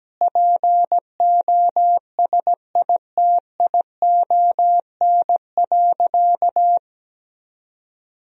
Morse code, heard as a keyed tone, about 17 wpm, 700 Hz. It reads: POSITION.
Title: Sentences from Top 1000 Words - Review Part 3 - 17wpm